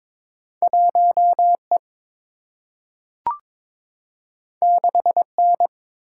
Morse code, heard as a keyed tone, about 22 wpm, 700 Hz. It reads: 1 E 6 N
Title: Single-Letter Single-Number 22wpm